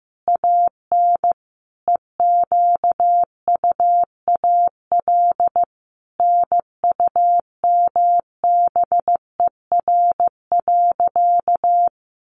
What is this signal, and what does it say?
Morse code, heard as a keyed tone, about 15 wpm, 700 Hz. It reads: AN EQUAL NUMBER.
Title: Sentences from Top 1400 Words 15wpm